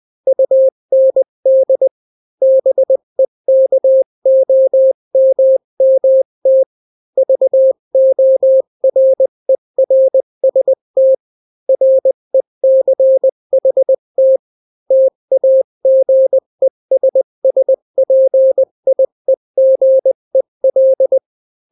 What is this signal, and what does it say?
Morse code, heard as a keyed tone, about 20 wpm, 530 Hz. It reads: UND BEKOMMT VORERST RECHT TAGESSPIEGEL
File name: daily / 18wpm